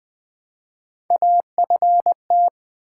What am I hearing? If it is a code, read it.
Morse code, heard as a keyed tone, about 20 wpm, 700 Hz. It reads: AFT
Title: Build A Word - 1 Letter At A Time - 3 to 5-Letter Words - 20wpm